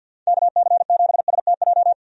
Morse code, heard as a keyed tone, about 52 wpm, 700 Hz. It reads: KC7STJ
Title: USA Call Signs - 6 Characters - 50wpm